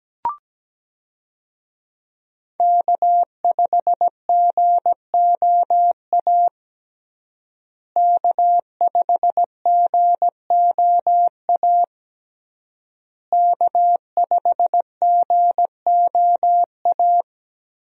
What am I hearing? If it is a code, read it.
Morse code, heard as a keyed tone, about 17 wpm, 700 Hz. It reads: K5GOA K5GOA K5GOA
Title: USA Call Signs - 5 Characters - Repeated 3x - 17wpm